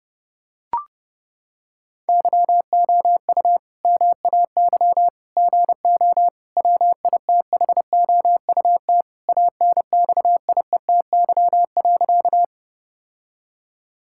Morse code, heard as a keyed tone, about 30 wpm, 700 Hz. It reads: YOU MAY GO WITHOUT ANXIETY.
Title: Sentences from Top 1600 Words 30wpm